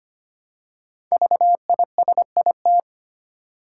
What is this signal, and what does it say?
Morse code, heard as a keyed tone, about 25 wpm, 700 Hz. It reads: VISIT
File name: 5 Letter Words - First 3 Letters Given - 25wpm